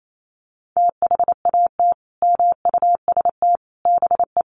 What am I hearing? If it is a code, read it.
Morse code, heard as a keyed tone, about 28 wpm, 700 Hz. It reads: THAT MUST BE
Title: Sentences from Top 400 Words 28wpm